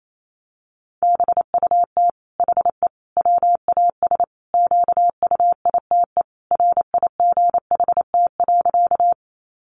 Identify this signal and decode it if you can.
Morse code, heard as a keyed tone, about 28 wpm, 700 Hz. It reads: BUT HE WAS QUITE RIGHT.
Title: Sentences from Top 700 Words 28wpm